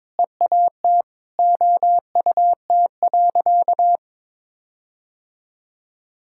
Morse code, heard as a keyed tone, about 22 wpm, 700 Hz. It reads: EAT OUT.
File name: practice / 22wpm / Sentences with 4-Word Phrases - 22wpm